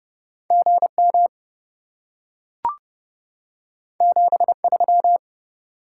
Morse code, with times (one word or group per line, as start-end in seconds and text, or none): 0.5-1.3 GM
4.0-5.2 73